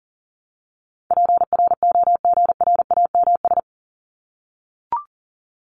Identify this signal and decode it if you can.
Morse code, heard as a keyed tone, about 40 wpm, 700 Hz. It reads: PROGRAMS
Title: ES:S Suffix - Top 100 Words with Base Words 40wpm